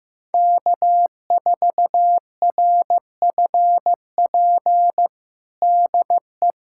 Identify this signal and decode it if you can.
Morse code, heard as a keyed tone, about 15 wpm, 700 Hz. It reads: K4RFP DE
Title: USA Callsign DE Callsign - 5 Characters - 15wpm